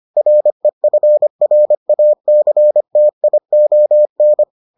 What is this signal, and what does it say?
Morse code, heard as a keyed tone, about 25 wpm, 600 Hz. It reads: REFRACTION